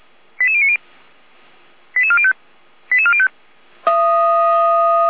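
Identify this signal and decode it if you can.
Five-tone selective call (ZVEI), heard as quick runs of five tones, tones of about 70 ms. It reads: 88090 88405 88405
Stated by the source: ZVEI five-tone call received on SDR, Feuerwehr Orlamünde.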